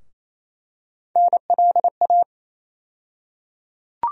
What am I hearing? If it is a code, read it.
Morse code, heard as a keyed tone, about 28 wpm, 700 Hz. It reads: NLA
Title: USA Call Sign Postfixes - 3 Letters 28wpm